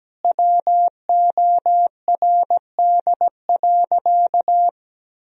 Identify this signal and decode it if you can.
Morse code, heard as a keyed tone, about 17 wpm, 700 Hz. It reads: WORD.
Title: Sentences from Top 1000 Words - Review Part 2 - 17wpm